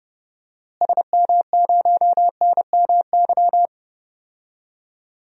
Morse code, heard as a keyed tone, about 29 wpm, 700 Hz. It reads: SM0NMY
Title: DXCC-100 Call Signs - 6 Characters - Repeated 2x - 30wpm